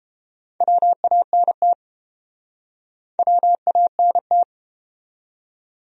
Morse code, heard as a keyed tone, about 30 wpm, 700 Hz. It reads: WANT WANT